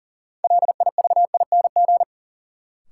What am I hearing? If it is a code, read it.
Morse code, heard as a keyed tone, about 40 wpm, 700 Hz. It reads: LIVING
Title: Sets of 4 Words - New Word Drill - 40wpm